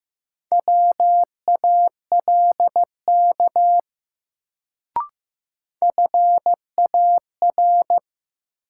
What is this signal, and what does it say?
Morse code, heard as a keyed tone, about 15 wpm, 700 Hz. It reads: WALK FAR
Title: Sentences from Top 300 Words - New Word Drill - 15wpm